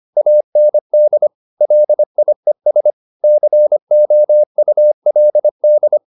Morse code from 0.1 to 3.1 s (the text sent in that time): AND LIES